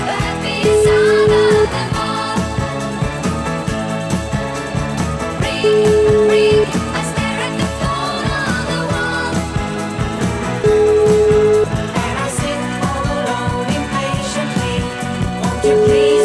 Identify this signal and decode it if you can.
Ringback tone (Europe, 425 Hz), with noise.